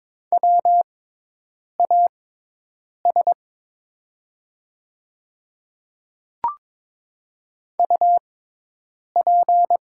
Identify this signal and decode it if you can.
Morse code, heard as a keyed tone, about 23 wpm, spaced as for 8 wpm, 700 Hz. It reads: WAS UP
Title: 6x Character Spacing - Top 100 Words - 22wpm